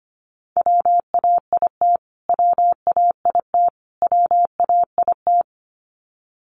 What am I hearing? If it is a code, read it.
Morse code, heard as a keyed tone, about 25 wpm, 700 Hz. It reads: WAIT WAIT WAIT
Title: Top 500 Words Repeated 25wpm